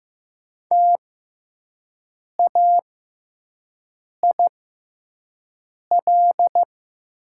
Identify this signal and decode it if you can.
Morse code, heard as a keyed tone, about 15 wpm, spaced as for 5 wpm, 690 Hz. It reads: TAIL
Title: 6x Character Spacing - Top 500 Words - 15wpm